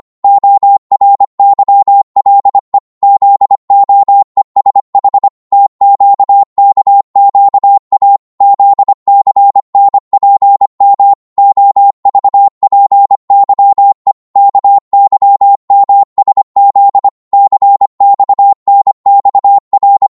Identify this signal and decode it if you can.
Morse code, heard as a keyed tone, about 25 wpm, 800 Hz. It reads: ORYLE ZOESH TQKQA ZCNPM OVPYE KYMSZ CXNXR